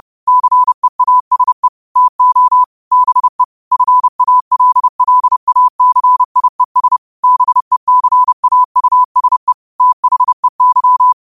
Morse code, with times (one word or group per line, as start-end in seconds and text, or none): 0.3-1.7 CEASE
2.0-2.6 TO
2.9-3.4 BE
3.7-7.0 FALLACIES
7.2-9.5 BECAUSE
9.8-11.1 THEY